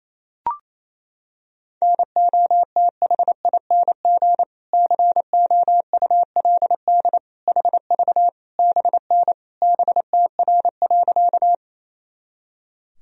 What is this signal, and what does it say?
Morse code, heard as a keyed tone, about 28 wpm, 700 Hz. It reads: NOTHING COULD HV BN BTR.
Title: Sentences from Top 500 Words and Abbreviations 28wpm